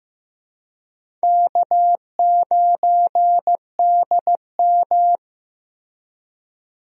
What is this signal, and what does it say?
Morse code, heard as a keyed tone, about 15 wpm, 700 Hz. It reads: K9DM
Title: USA Call Signs - 4 Characters - Repeated 3x - 15wpm